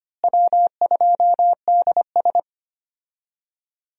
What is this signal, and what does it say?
Morse code, heard as a keyed tone, about 25 wpm, 700 Hz. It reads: W2DS
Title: USA Call Signs - 4 Characters - Repeated 3x - 25wpm